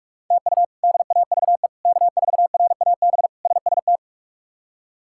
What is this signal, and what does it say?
Morse code, heard as a keyed tone, about 46 wpm, 700 Hz. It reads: TU DAVE K4RAB SST
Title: K1USN SST QSO Practice - 45wpm